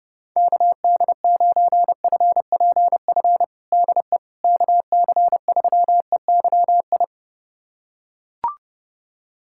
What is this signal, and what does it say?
Morse code, heard as a keyed tone, about 30 wpm, 700 Hz. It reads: KD9FPF DE KC3EYI
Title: USA Callsign DE Callsign - 6 Characters - 30wpm